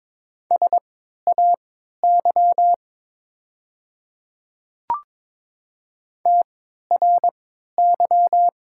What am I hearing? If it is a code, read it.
Morse code, heard as a keyed tone, about 22 wpm, spaced as for 12 wpm, 700 Hz. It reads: SAY TRY
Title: 3x Character Spacing - Top 200 Words - 22wpm